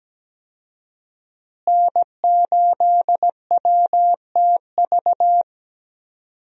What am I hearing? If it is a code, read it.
Morse code, heard as a keyed tone, about 17 wpm, 700 Hz. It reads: N8WTV
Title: USA Call Signs - 5 Characters - Repeated 2x - 17wpm